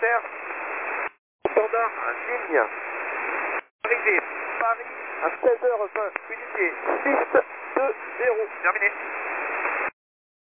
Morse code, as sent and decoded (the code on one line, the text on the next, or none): none